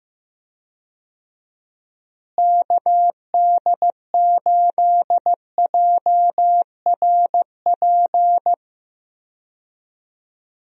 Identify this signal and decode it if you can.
Morse code, heard as a keyed tone, about 15 wpm, 700 Hz. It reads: KD8JRP
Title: USA Call Signs - 6 Characters - Repeated 3x - 15wpm